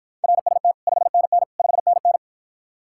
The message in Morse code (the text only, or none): RST 5NN 5NN